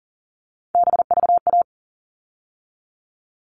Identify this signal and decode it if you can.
Morse code, heard as a keyed tone, about 40 wpm, 700 Hz. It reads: BVA
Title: All Cut Numbers - 3-Digit 40wpm